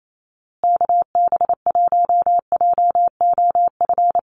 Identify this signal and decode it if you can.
Morse code, heard as a keyed tone, about 28 wpm, 700 Hz. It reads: KB1JOF
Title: USA Call Signs Repeated 2x 28wpm